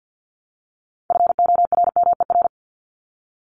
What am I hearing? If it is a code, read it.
Morse code, heard as a keyed tone, about 50 wpm, 700 Hz. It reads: FORMER
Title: Top 1751-2000 Words 50wpm